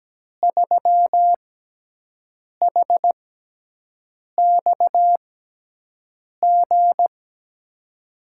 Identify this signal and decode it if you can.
Morse code, heard as a keyed tone, about 17 wpm, spaced as for 6 wpm, 700 Hz. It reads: 3HXG
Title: USA Call Signs - 5 Characters - 6x Character Spacing - 17wpm